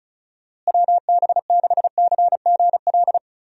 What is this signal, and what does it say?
Morse code, heard as a keyed tone, about 36 wpm, 700 Hz. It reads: WB6CGL
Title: USA Call Signs - 6 Characters - Repeated 2x - 35wpm